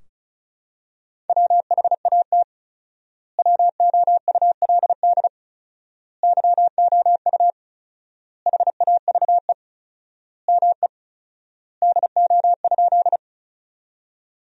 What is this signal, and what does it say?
Morse code, heard as a keyed tone, about 34 wpm, 700 Hz. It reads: WHAT WOULD YOU HAVE ME DO?
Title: 4x Word-Spacing - Sentences from Top 200 Words - 35wpm